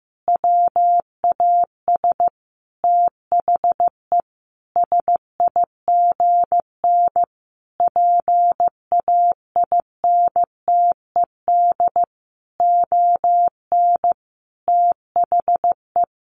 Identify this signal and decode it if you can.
Morse code, heard as a keyed tone, about 15 wpm, 700 Hz. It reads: WAS THE SIGN PAINTED ON THE
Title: A Sound Of Thunder 15wpm